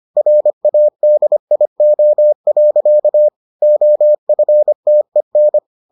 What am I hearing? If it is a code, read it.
Morse code, heard as a keyed tone, about 25 wpm, 600 Hz. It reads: RADIO. OFTEN